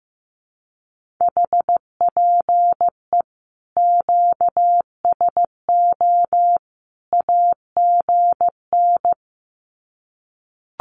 Morse code, heard as a keyed tone, about 15 wpm, 700 Hz. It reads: HPE QSO AGN
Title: Common QSO Phrases 15wpm